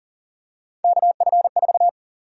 Morse code, heard as a keyed tone, about 40 wpm, 700 Hz. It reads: KF4
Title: USA Call Sign Prefixes - 40wpm